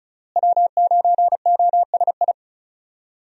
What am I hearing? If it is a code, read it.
Morse code, heard as a keyed tone, about 36 wpm, 700 Hz. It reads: W9OSI
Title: USA Call Signs - 5 Characters - Repeated 3x - 35wpm